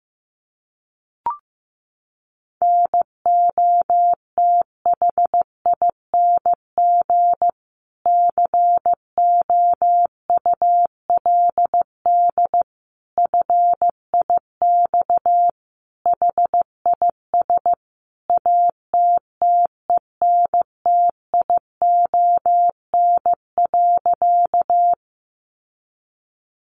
Morse code, heard as a keyed tone, about 15 wpm, 700 Hz. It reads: NOTHING COULD FIX HIS ATTENTION.
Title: Sentences from Top 1400 Words 15wpm